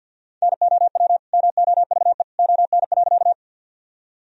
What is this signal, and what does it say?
Morse code, heard as a keyed tone, about 51 wpm, 700 Hz. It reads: NOW MOVE ON.
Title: Sentences from Top 1000 Words - Review Part 1 - 50wpm